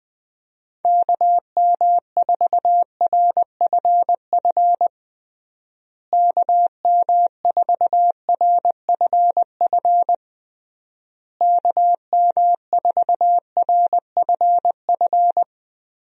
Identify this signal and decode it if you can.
Morse code, heard as a keyed tone, about 20 wpm, 700 Hz. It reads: KM4RFF KM4RFF KM4RFF
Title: USA Call Signs - 6 Characters - Repeated 3x - 20wpm